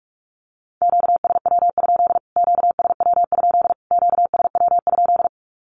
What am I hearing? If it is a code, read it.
Morse code, heard as a keyed tone, about 44 wpm, 700 Hz. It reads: QSW? QSW? QSW?
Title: Q-Codes Repeated 3x 45wpm